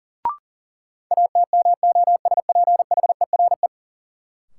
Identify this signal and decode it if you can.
Morse code, heard as a keyed tone, about 40 wpm, 700 Hz. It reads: ATMOSPHERE
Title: Sentences from Top 1600 Words - New Word Drill Alt - 40wpm